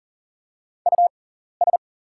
Morse code, heard as a keyed tone, about 41 wpm, spaced as for 14 wpm, 690 Hz. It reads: US
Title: 6x Character Spacing - Top 500 Words - 40wpm